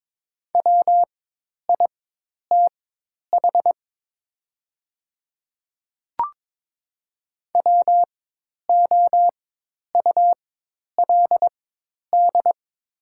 Morse code, heard as a keyed tone, about 22 wpm, spaced as for 10 wpm, 700 Hz. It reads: WITH WOULD
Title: 4x Character Spacing - Top 500 Words - 22wpm